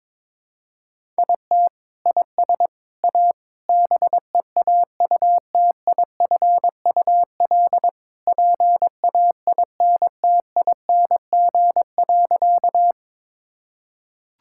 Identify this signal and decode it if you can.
Morse code, heard as a keyed tone, about 22 wpm, 700 Hz. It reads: IT IS A BEAUTIFUL PAINTING.
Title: Sentences from Top 1000 Words - Review Part 6 - 22wpm